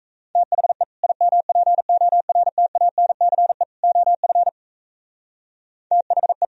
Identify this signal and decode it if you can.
Morse code, heard as a keyed tone, about 34 wpm, 700 Hz. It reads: THE IMPORTANCE OF THE
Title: Speed-Racing - Sets of 3 Words 28wpm